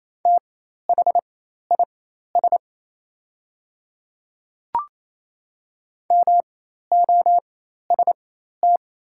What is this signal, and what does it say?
Morse code, heard as a keyed tone, about 28 wpm, spaced as for 13 wpm, 700 Hz. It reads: THIS MOST
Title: 4x Character Spacing - Top 100 Words - 28wpm